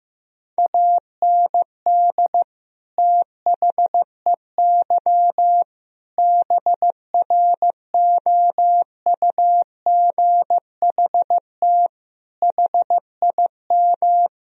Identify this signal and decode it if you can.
Morse code, heard as a keyed tone, about 15 wpm, 700 Hz. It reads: AND THEY BROUGHT HIM
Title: Sentences from Top 1000 Words - Review Part 2 - 15wpm